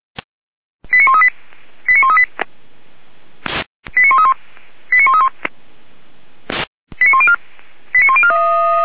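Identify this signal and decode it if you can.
Five-tone selective call (ZVEI), heard as quick runs of five tones, tones of about 80 ms. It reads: 89138 89138 89131 89131 89104 89104
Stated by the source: ZVEI five-tone call received by SDR, archived by Feuerwehr Orlamünde.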